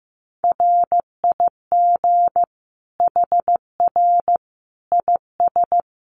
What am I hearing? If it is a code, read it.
Morse code, heard as a keyed tone, about 15 wpm, 700 Hz. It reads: RIG HR IS
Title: Common QSO Phrases 15wpm